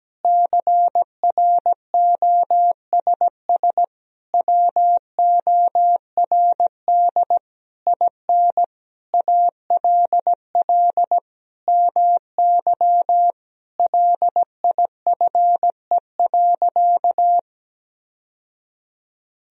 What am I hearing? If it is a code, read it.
Morse code, heard as a keyed tone, about 17 wpm, 700 Hz. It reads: CROSS WORD IN ALL MY LIFE.
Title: Sentences from Top 1000 Words - Review Part 1 - 17wpm